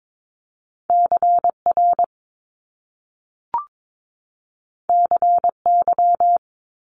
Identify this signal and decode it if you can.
Morse code, heard as a keyed tone, about 22 wpm, 700 Hz. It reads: CR CY
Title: Top Two Letter Combinations in Words 22wpm